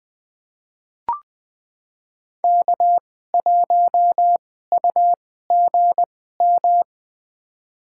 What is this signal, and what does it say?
Morse code, heard as a keyed tone, about 20 wpm, spaced as for 14 wpm, 700 Hz. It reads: K1UGM
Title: USA Call Signs - 5 Characters - 2x Character Spacing - 20wpm